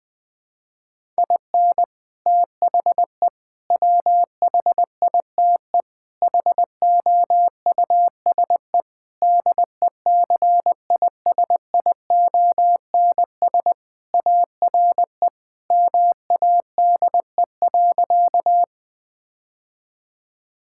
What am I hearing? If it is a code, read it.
Morse code, heard as a keyed tone, about 20 wpm, 700 Hz. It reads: IN THE WHITE HOUSE DECISIONS ARE MADE.
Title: Sentences with 4-Word Phrases - 20wpm